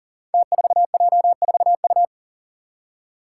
Morse code, heard as a keyed tone, about 37 wpm, 700 Hz. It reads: T4J4U
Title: Five Challenging Letter-Number Combinations 40wpm